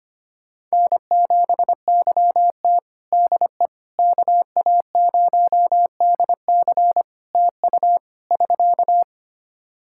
Morse code, heard as a keyed tone, about 25 wpm, 700 Hz. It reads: N7YT DE KA0DC TU <SK>
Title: Practice QSO - Exchange 4 - Ending - 25wpm